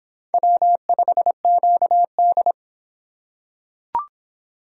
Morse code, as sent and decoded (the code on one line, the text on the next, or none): .-- ..... --.- -..
W5QD